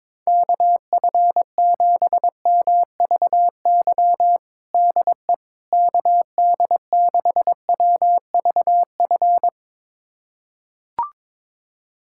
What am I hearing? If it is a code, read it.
Morse code, heard as a keyed tone, about 22 wpm, 700 Hz. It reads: KF7MVY DE KD6WVF
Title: USA Callsign DE Callsign - 6 Characters - 22wpm